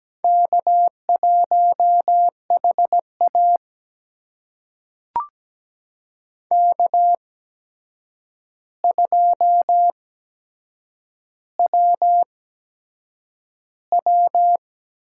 Morse code, heard as a keyed tone, about 17 wpm, spaced as for 7 wpm, 700 Hz. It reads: K1HA K2WW
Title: USA Call Signs - 4 Characters - 8x Character Spacing - 17wpm